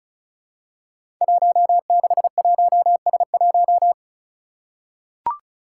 Morse code, heard as a keyed tone, about 34 wpm, 700 Hz. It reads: 161S1
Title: Five Challenging Letter-Number Combinations 35wpm